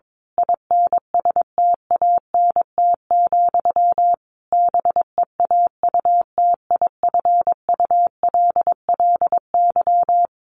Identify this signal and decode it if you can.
Morse code, heard as a keyed tone, about 22 wpm, 700 Hz. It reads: INSTANT, BEAUTIFULLY
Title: A Sound Of Thunder 22wpm